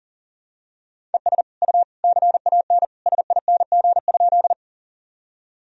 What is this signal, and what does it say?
Morse code, heard as a keyed tone, about 37 wpm, 700 Hz. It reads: ES U CAN SING?
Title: Sentences from Top 400 Words and Abbreviations 40wpm